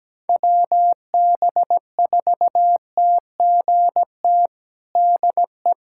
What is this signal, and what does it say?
Morse code, heard as a keyed tone, about 17 wpm, 700 Hz. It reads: WB4TGT DE
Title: USA Callsign DE Callsign - 6 Characters - 17wpm